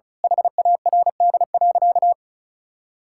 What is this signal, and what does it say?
Morse code, heard as a keyed tone, about 34 wpm, 700 Hz. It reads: HARD.
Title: Pangram Challenge v2 - 35wpm